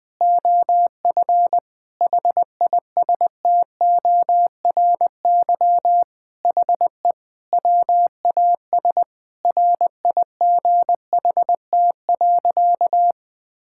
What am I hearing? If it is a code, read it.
Morse code, heard as a keyed tone, about 21 wpm, 700 Hz. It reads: OF HISTORY HE WAS RIGHT.
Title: Sentences with 4-Word Phrases - Spoken Beginning - 20wpm